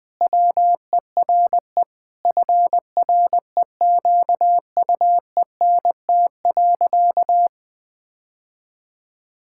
Morse code, heard as a keyed tone, about 20 wpm, 700 Hz. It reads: WERE FREQUENT.
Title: Sentences with 6-Word Phrases - Spoken Beginning - 20wpm